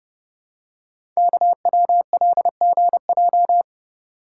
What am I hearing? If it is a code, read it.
Morse code, heard as a keyed tone, about 30 wpm, 700 Hz. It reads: KWLGJ
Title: Code Groups - 5 Letters - 30wpm